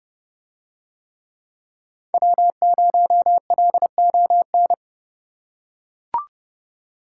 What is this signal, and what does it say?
Morse code, heard as a keyed tone, about 29 wpm, 700 Hz. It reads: W0LON